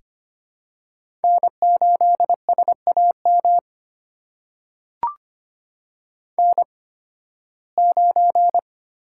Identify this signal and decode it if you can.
Morse code, heard as a keyed tone, about 25 wpm, spaced as for 16 wpm, 700 Hz. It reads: N8SAM N9
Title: USA Call Signs - 5 Characters - 8x Character Spacing - 25wpm